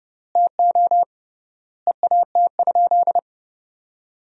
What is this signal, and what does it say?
Morse code, heard as a keyed tone, about 30 wpm, 700 Hz. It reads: TO EAT?
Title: 3x Word-Spacing - Sentences from Top 300 Words - 30wpm